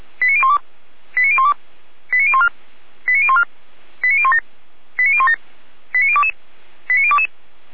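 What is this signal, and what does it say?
Five-tone selective call (ZVEI), heard as quick runs of five tones, tones of about 80 ms. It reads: 89012 89012 89014 89014 89017 89017 89022 89022
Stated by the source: Five-tone ZVEI selective call received on an SDR.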